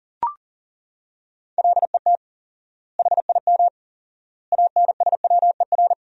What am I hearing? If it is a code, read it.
Morse code, heard as a keyed tone, about 40 wpm, 700 Hz. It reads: LET HIM ANSWER
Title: 4x Word-Spacing - Sentences from Top 300 Words - 40wpm